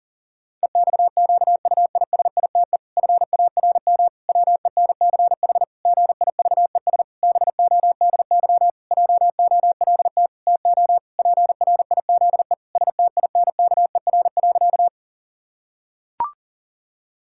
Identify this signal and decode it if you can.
Morse code, heard as a keyed tone, about 40 wpm, 700 Hz. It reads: EXQUISITE FARM WENCH GIVES BODY JOLT TO PRIZE STINKER.
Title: Pangram Challenge v2 - 40wpm